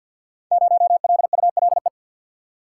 Morse code, heard as a keyed tone, about 48 wpm, 700 Hz. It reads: 0LULE
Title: Code Groups - 5 Letter:Numbers - 50wpm